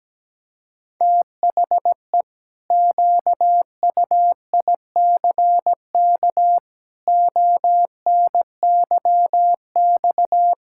Morse code, heard as a keyed tone, about 17 wpm, 700 Hz. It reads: THE QUICK ONYX
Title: Pangram Challenge v2 - 17wpm